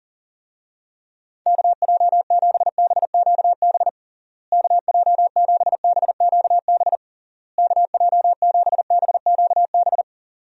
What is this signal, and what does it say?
Morse code, heard as a keyed tone, about 40 wpm, 700 Hz. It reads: KJ7BQB KJ7BQB KJ7BQB
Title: USA Call Signs - 6 Characters - Repeated 3x - 40wpm